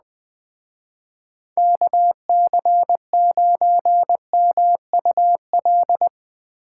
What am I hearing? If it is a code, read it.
Morse code, heard as a keyed tone, about 20 wpm, 700 Hz. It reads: KC9MUL
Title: Speed Racing - USA 2x3 Call Signs 17wpm